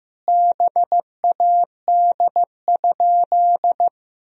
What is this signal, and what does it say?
Morse code, heard as a keyed tone, about 15 wpm, 700 Hz. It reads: BAD?
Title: Sentences from Top 1000 Words - Review Part 1 - 15wpm